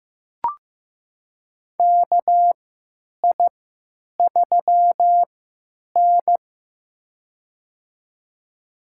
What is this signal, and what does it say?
Morse code, heard as a keyed tone, about 15 wpm, spaced as for 9 wpm, 700 Hz. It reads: KI3N
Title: USA Call Signs - 4 Characters - 3x Character Spacing - 15wpm